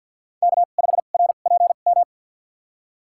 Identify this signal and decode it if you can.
Morse code, heard as a keyed tone, about 47 wpm, spaced as for 35 wpm, 700 Hz. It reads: K5RPM
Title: USA Call Signs - 5 Characters - 2x Character Spacing - 50wpm